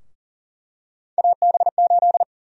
Morse code, heard as a keyed tone, about 40 wpm, 700 Hz. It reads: AB8
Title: USA Call Sign Prefixes - Repeated 2x - 40wpm